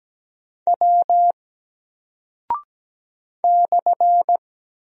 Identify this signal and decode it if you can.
Morse code, heard as a keyed tone, about 17 wpm, 700 Hz. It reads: W /
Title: Single Character (Letter:Number:Punctuation:Prosign) 17wpm